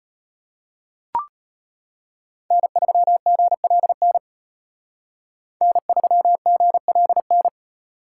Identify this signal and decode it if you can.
Morse code, heard as a keyed tone, about 32 wpm, 700 Hz. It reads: N3GLN N3GLN